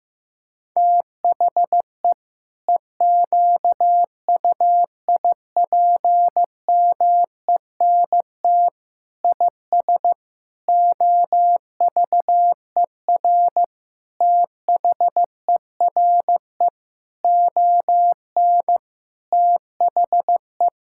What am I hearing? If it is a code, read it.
Morse code, heard as a keyed tone, about 15 wpm, 700 Hz. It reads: THE EQUIPMENT IS OVER THERE ON THE
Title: Sentences from Top 1000 Words - Review Part 5 - 15wpm